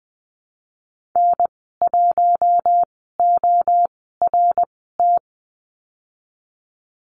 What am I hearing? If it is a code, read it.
Morse code, heard as a keyed tone, about 20 wpm, spaced as for 15 wpm, 700 Hz. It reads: N1ORT
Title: 055 Lesson 10g - Letter r - Call Signs 20wpm